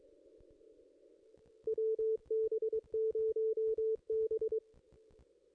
Morse code, heard as a keyed tone, about 23 wpm, 440 Hz. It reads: WB0B
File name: contests / ss-cw-k5tr-op-n5rz -1700